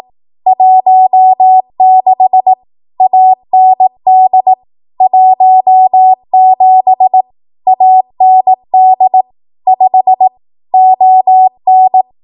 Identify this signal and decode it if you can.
Morse code, heard as a keyed tone, about 18 wpm, 750 Hz. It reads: 16 AND 17 AND 5 ON
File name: Archive / 18 WPM / 18WPM